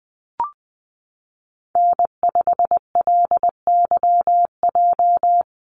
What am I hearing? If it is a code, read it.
Morse code, heard as a keyed tone, about 20 wpm, 700 Hz. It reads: N5LYJ
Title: Call Signs Repeated 2x - 20wpm